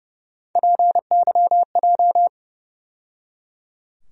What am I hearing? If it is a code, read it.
Morse code, heard as a keyed tone, about 30 wpm, 700 Hz. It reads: PYJ
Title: USA Call Sign Postfixes - 3 Letters 30wpm